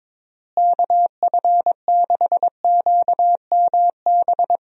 Morse code, heard as a keyed tone, about 22 wpm, 700 Hz. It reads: KF6QMB
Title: USA Call Signs - 6 Characters - Repeated 3x - 22wpm